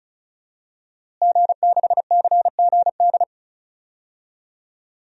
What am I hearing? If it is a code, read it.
Morse code, heard as a keyed tone, about 36 wpm, 700 Hz. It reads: G6CGD
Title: DXCC-100 Call Signs - 5 Characters - Repeated 2x - 35wpm